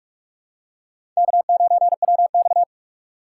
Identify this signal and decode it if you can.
Morse code, heard as a keyed tone, about 43 wpm, 700 Hz. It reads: K9WX
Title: USA Call Signs - 4 Characters - 4x Character Spacing - 45wpm